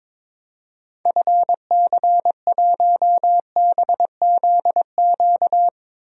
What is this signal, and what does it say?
Morse code, heard as a keyed tone, about 22 wpm, 700 Hz. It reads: FC1BZQ
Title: DXCC-100 Call Signs - 6 Characters - Repeated 2x - 22wpm